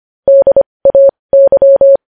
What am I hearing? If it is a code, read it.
Morse code, heard as a keyed tone, about 25 wpm, spaced as for 20 wpm, 550 Hz. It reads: DAY
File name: podcast / News headlines-1x-25x20